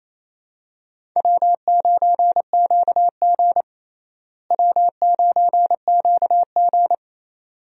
Morse code, heard as a keyed tone, about 28 wpm, 700 Hz. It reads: W9QG W9QG
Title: USA Call Signs - 4 Characters - Repeated 2x - 28wpm